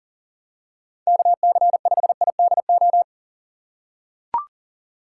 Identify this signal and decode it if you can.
Morse code, heard as a keyed tone, about 40 wpm, 700 Hz. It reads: KC5IDO
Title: USA Call Signs - 6 Characters - 3x Character Spacing - 40wpm